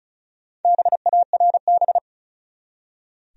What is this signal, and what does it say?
Morse code, heard as a keyed tone, about 34 wpm, 700 Hz. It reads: BARB
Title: USA Common Names - Max 5 Letters - 35wpm